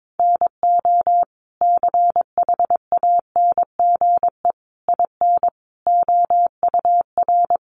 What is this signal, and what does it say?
Morse code, heard as a keyed tone, about 22 wpm, 700 Hz. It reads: NO CHANGE IN OUR
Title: Sentences from Top 1200 Words 22wpm